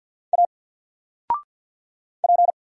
Morse code, heard as a keyed tone, about 48 wpm, 700 Hz. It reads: A P
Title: Single Character (Letter:Number:Punctuation:Prosign) 50wpm